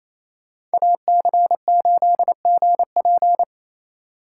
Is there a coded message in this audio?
Yes: Morse code from 0.7 to 3.4 s, heard as a keyed tone, about 36 wpm, 700 Hz. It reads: AC8GP